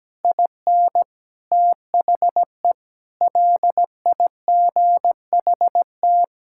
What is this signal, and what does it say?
Morse code, heard as a keyed tone, about 17 wpm, 700 Hz. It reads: IN THE LIGHT
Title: Sentences from Top 1000 Words - Review Part 5 - 17wpm